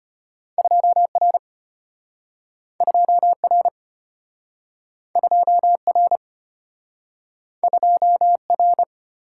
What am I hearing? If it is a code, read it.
Morse code, heard as a keyed tone, about 34 wpm, 700 Hz. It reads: 2R 2R 2R 2R